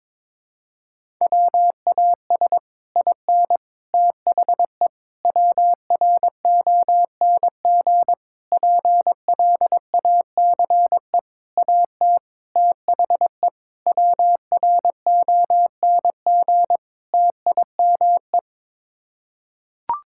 Morse code, heard as a keyed tone, about 22 wpm, 700 Hz. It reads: WAS IN THE WRONG PLACE AT THE WRONG TIME
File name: Sets of 9 Words - 22wpm